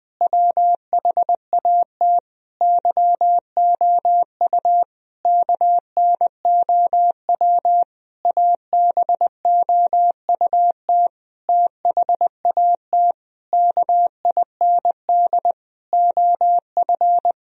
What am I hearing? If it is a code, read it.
Morse code, heard as a keyed tone, about 20 wpm, 700 Hz. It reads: WHAT YOU KNOW ABOUT THAT KIND OF
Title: Sentences from Top 1000 Words - Review Part 4 - 20wpm